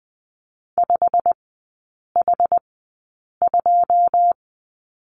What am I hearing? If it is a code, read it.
Morse code, heard as a keyed tone, about 20 wpm, spaced as for 8 wpm, 700 Hz. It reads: 5H2
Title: 087 Lesson 15e - Number 5 - Three Characters 20wpm